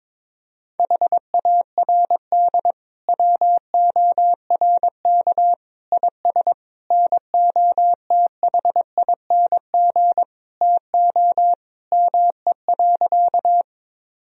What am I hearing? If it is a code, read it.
Morse code, heard as a keyed tone, about 22 wpm, 700 Hz. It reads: HARD WORK IS NOTHING TO ME.
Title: Sentences from Top 1000 Words - Review Part 1 - 22wpm